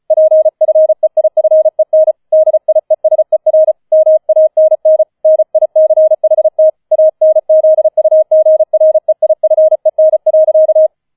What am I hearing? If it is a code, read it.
Morse code, heard as a keyed tone, about 34 wpm, 630 Hz. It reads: PFEIFEN DIESER MANN NICHT ANZUGREIFEN.
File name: 004 Vierter Streich 35wpm